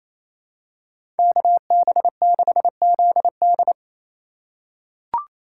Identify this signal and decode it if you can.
Morse code, heard as a keyed tone, about 28 wpm, 700 Hz. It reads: KB6ZD
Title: USA Call Signs - 5 Characters - 4x Character Spacing - 28wpm